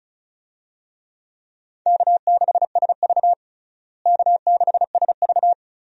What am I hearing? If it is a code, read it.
Morse code, heard as a keyed tone, about 34 wpm, 700 Hz. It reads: K6SV K6SV
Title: USA Call Signs - 4 Characters - Repeated 2x - 35wpm